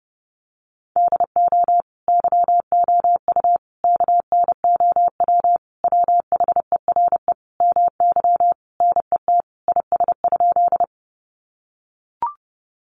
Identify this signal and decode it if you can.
Morse code, heard as a keyed tone, about 30 wpm, 700 Hz. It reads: DO YOU KNOW WHERE MY NET IS?
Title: Sentences from Top 1400 Words 30wpm